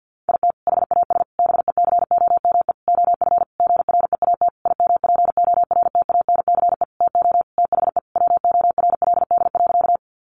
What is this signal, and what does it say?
Morse code, heard as a keyed tone, about 50 wpm, 700 Hz. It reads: IT HAS BECOME OF GREAT IMPORTANCE TO THE WORLD.
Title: Sentences from Top 1200 Words 50wpm